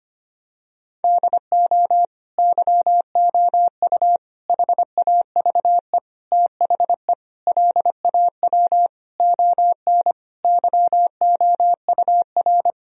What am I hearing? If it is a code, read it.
Morse code, heard as a keyed tone, about 25 wpm, 700 Hz. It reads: DO YOU HAVE THE LAW ON YOUR
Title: Sentences from Top 1000 Words - Review Part 3 - 25wpm